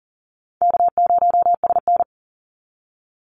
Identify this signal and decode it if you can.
Morse code, heard as a keyed tone, about 40 wpm, 700 Hz. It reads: K0SN
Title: Easy Call Signs Repeated 3x - 40wpm